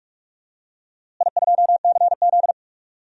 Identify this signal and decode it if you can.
Morse code, heard as a keyed tone, about 41 wpm, 700 Hz. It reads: I2CZ
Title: DXCC-100 Call Signs - 4 Characters - Repeated 3x - 45wpm